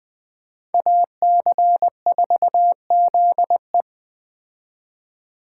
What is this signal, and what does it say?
Morse code, heard as a keyed tone, about 20 wpm, 700 Hz. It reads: AC4ZE